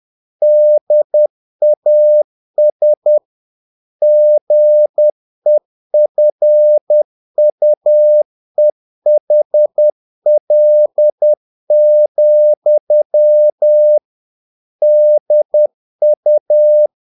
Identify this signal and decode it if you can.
Morse code, heard as a keyed tone, about 10 wpm, 600 Hz. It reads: DAS GEFUEHL, DU
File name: dk5ke / Humor-50BpM